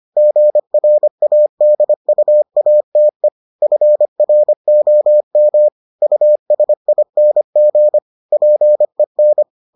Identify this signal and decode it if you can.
Morse code, heard as a keyed tone, about 25 wpm, 600 Hz. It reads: GRADUATE FROM USING PEN